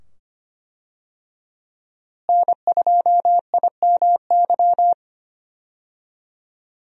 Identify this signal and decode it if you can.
Morse code, heard as a keyed tone, about 25 wpm, 690 Hz. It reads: N2IMY
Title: USA Call Signs - 5 Characters - Repeated 2x - 25wpm